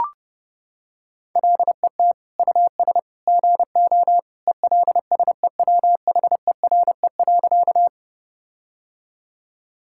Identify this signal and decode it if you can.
Morse code, heard as a keyed tone, about 32 wpm, 700 Hz. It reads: LET US GO ELSEWHERE.
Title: Sentences from Top 1600 Words 30wpm